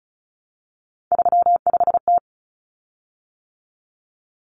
Morse code, heard as a keyed tone, about 34 wpm, 700 Hz. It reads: 35T
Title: Common Cut Numbers - 3-Digit 35wpm